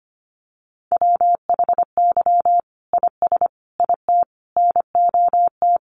Morse code, heard as a keyed tone, about 25 wpm, 700 Hz. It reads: WHY IS IT NOT
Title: Sentences from Top 1400 Words 25wpm